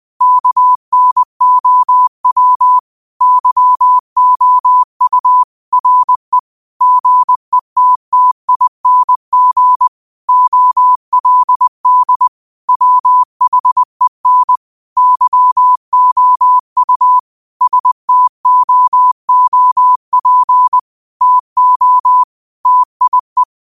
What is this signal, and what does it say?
Morse code, heard as a keyed tone, about 20 wpm, 1000 Hz. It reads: KNOW YOU RE GETTING OLD WHEN YOU STOOP TO TIE
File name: qotdfiles / 20wpm